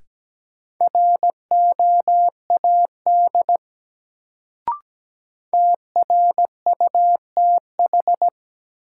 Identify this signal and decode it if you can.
Morse code, heard as a keyed tone, about 17 wpm, 700 Hz. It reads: ROAD TRUTH
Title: Sets of 5 Words - New Word Drill - 17wpm